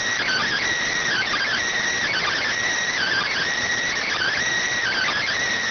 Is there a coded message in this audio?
Yes: Emergency Alert System (SAME) tones (header bursts) from 1.1 to 5.4 s, heard as harsh buzzing bursts.